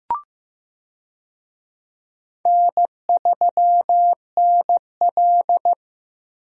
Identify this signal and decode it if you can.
Morse code, heard as a keyed tone, about 15 wpm, 700 Hz. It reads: N3NL
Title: USA Call Signs - 4 Characters - Repeated 2x - 15wpm